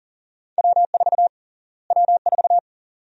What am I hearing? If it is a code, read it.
Morse code, heard as a keyed tone, about 40 wpm, 700 Hz. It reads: W4 W4
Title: USA Call Sign Prefixes - Repeated 2x - 40wpm